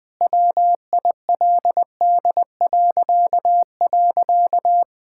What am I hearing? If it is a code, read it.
Morse code, heard as a keyed tone, about 20 wpm, 700 Hz. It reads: WILD..
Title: Pangram Challenge v2 - 20wpm